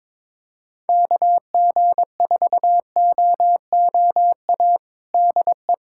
Morse code, heard as a keyed tone, about 21 wpm, 700 Hz. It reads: KG4OOA DE
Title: USA Callsign DE Callsign 22wpm